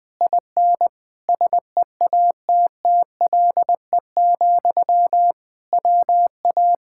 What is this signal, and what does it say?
Morse code, heard as a keyed tone, about 20 wpm, 700 Hz. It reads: IN SEATTLE, WA
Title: Speed-Racing Common QSO Phrases - 17wpm